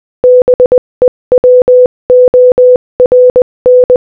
Morse code, heard as a keyed tone, about 20 wpm, 500 Hz. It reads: BE WORN
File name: Morse-Code-Audio-Speed-20-7